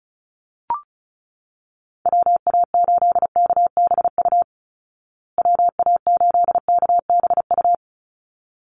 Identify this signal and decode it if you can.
Morse code, heard as a keyed tone, about 34 wpm, 700 Hz. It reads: WA8KBU WA8KBU
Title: Call Signs Repeated 3x - 35wpm